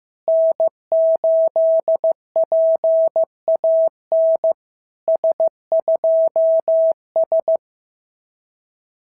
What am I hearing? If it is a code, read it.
Morse code, heard as a keyed tone, about 15 wpm, 650 Hz. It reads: N8PAN S2S
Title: SOTA QSO Practice 15wpm